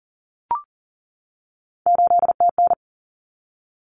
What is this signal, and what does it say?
Morse code, heard as a keyed tone, about 40 wpm, 700 Hz. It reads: 8TN
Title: Common Cut Numbers - 3-Digit 40wpm